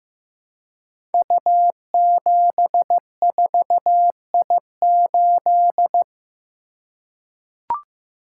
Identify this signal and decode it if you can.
Morse code, heard as a keyed tone, about 15 wpm, 700 Hz. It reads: U74I8
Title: Code Groups - 5 Letter:Numbers - 15wpm